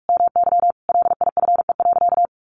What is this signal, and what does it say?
Morse code, heard as a keyed tone, about 44 wpm, 700 Hz. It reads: MY LIFE.
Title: Sentences from Top 1200 Words 45wpm